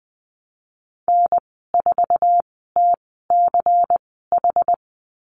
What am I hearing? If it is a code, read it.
Morse code, heard as a keyed tone, about 20 wpm, spaced as for 14 wpm, 700 Hz. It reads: N4TCH
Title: 238 Lesson 38g - Letter q - Call Signs 20wpm